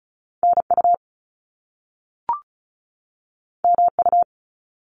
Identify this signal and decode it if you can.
Morse code, heard as a keyed tone, about 36 wpm, 700 Hz. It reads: NU MU
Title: Top Two Letter Combinations in Words 35wpm